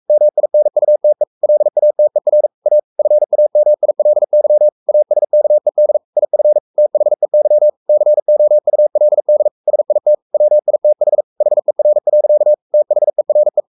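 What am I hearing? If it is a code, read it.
Morse code, heard as a keyed tone, about 44 wpm, 600 Hz. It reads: MINUTE LATER A FAMILY ASKED IF THEY COULD SIT WITH HER. THERE
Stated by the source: Morse code at 43 wpm sending Ken Follett's novel Never in English, from CW med Gnister.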